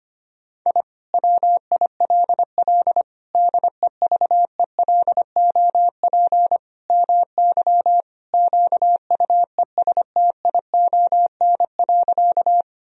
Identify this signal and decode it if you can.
Morse code, heard as a keyed tone, about 25 wpm, 700 Hz. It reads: I WILL DEVELOP MY QUESTION.
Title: Sentences from Top 1000 Words - Review Part 2 - 25wpm